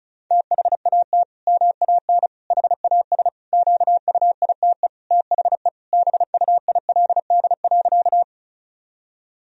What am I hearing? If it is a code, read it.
Morse code, heard as a keyed tone, about 36 wpm, 700 Hz. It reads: THAT MAN HAS QUITE THE BUILD.
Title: Sentences from Top 1000 Words - Review Part 1 - 35wpm